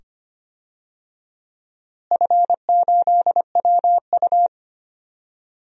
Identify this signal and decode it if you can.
Morse code, heard as a keyed tone, about 25 wpm, 700 Hz. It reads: F8WU
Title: DXCC-100 Call Signs - 4 Characters - Repeated 2x - 25wpm